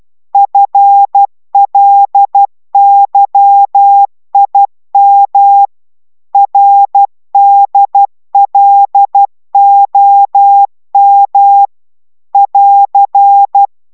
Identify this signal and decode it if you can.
Morse code, heard as a keyed tone, about 12 wpm, 800 Hz. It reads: FLYIM RDLOM <AR>